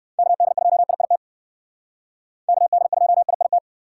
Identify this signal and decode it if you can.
Morse code, heard as a keyed tone, about 58 wpm, 700 Hz. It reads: KD2IIT KD2IIT